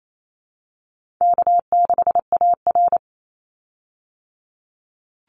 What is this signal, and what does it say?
Morse code, heard as a keyed tone, about 28 wpm, 700 Hz. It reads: K6AR
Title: Call Signs Repeated 2x - 28wpm